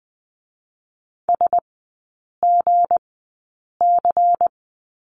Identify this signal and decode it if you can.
Morse code, heard as a keyed tone, about 20 wpm, spaced as for 8 wpm, 700 Hz. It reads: SGC
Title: 159 Lesson 26e - Letter g - Three Characters 20wpm